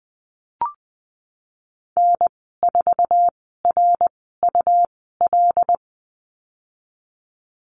Morse code, heard as a keyed tone, about 20 wpm, spaced as for 15 wpm, 700 Hz. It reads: N4RUL
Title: 186 Lesson 30g - Letter b - Call Signs 20wpm